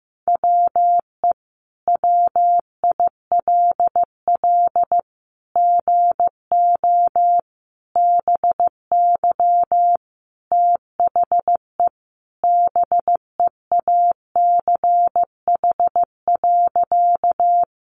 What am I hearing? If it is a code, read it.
Morse code, heard as a keyed tone, about 15 wpm, 700 Hz. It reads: WE WILL GO BY THE BEACH.
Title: Sentences from Top 1500 Words 15wpm